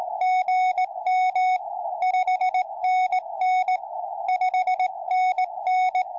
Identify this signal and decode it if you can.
Morse code, heard as a keyed tone, about 17 wpm, 740 Hz. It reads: GM 5NN 5NN